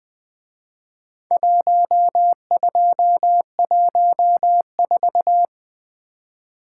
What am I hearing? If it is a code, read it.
Morse code, heard as a keyed tone, about 22 wpm, 700 Hz. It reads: 1214
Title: Speed-Racing - Four Letter-Number Combinations - 20wpm